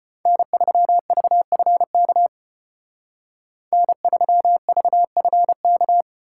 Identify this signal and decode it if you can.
Morse code, heard as a keyed tone, about 33 wpm, 700 Hz. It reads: N3VFK N3VFK